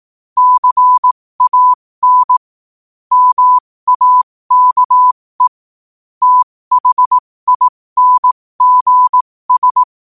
Morse code, heard as a keyed tone, about 18 wpm, spaced as for 15 wpm, 1000 Hz. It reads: CAN MAKE THINGS
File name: qotdfiles / 15wpm